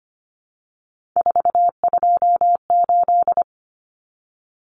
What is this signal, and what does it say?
Morse code, heard as a keyed tone, about 25 wpm, 700 Hz. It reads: 428
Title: Common Cut Numbers - 3-Digit 25wpm